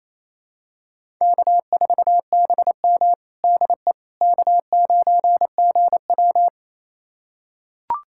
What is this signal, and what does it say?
Morse code, heard as a keyed tone, about 28 wpm, 700 Hz. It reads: K4BM DE K9GW
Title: USA Callsign DE Callsign - 4 Characters - 28wpm